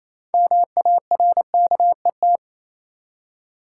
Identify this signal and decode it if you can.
Morse code, heard as a keyed tone, about 28 wpm, 700 Hz. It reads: MARKET
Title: Six Letter Words - 28wpm